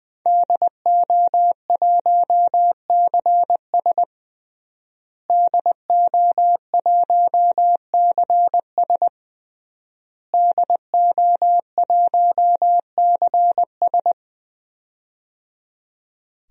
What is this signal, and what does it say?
Morse code, heard as a keyed tone, about 21 wpm, 700 Hz. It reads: DO1CS DO1CS DO1CS
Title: DXCC-100 Call Signs - 5 Characters - Repeated 3x - 20wpm